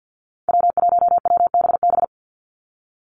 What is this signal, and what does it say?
Morse code, heard as a keyed tone, about 50 wpm, 700 Hz. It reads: W1WBB
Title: Call Signs Repeated 2x - 50wpm